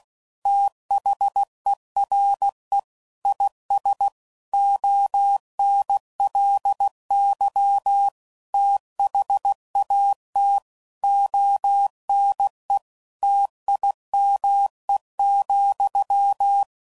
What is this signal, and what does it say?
Morse code, heard as a keyed tone, about 16 wpm, 800 Hz. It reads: THERE IS ONLY THAT ONE TIME,
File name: CW Code Groups @15WPM / Technical Subjects / AND Gate